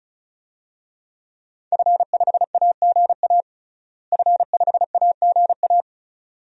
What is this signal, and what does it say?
Morse code, heard as a keyed tone, about 34 wpm, 700 Hz. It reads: F5AGA F5AGA
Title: DXCC-100 Call Signs - 5 Characters - Repeated 2x - 35wpm